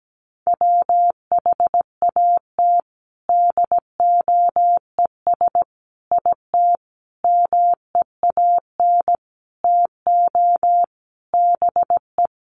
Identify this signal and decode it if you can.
Morse code, heard as a keyed tone, about 17 wpm, 700 Hz. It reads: WHAT DOES IT MEAN TO BE
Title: Sentences from Top 1500 Words 17wpm